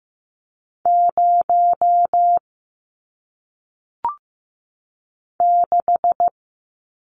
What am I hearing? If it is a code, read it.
Morse code, heard as a keyed tone, about 15 wpm, 700 Hz. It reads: 0 6
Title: Numbers - 1-Digit 15wpm